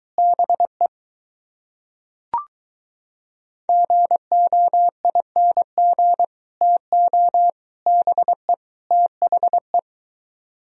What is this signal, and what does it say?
Morse code, heard as a keyed tone, about 23 wpm, 700 Hz. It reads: BE GOING TO BE THE